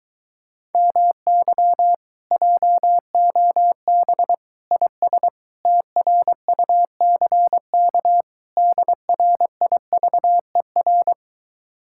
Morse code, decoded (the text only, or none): MY JOB IS TRUCK DRIVER